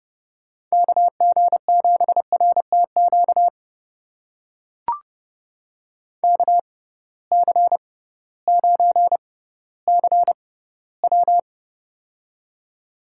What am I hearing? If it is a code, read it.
Morse code, heard as a keyed tone, about 31 wpm, spaced as for 16 wpm, 700 Hz. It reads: KG7RTQ KC9CW
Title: USA Call Signs - 6x Character Spacing - 30wpm